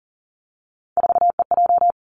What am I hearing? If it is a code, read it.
Morse code, heard as a keyed tone, about 41 wpm, 700 Hz. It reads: 4EJ
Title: Three Challenging Letter-Number Combinations 40wpm